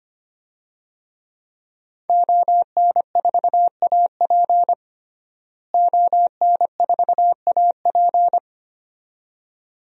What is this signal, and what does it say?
Morse code, heard as a keyed tone, about 25 wpm, 700 Hz. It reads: ON4AP ON4AP
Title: DXCC-100 Call Signs - 5 Characters - Repeated 2x - 25wpm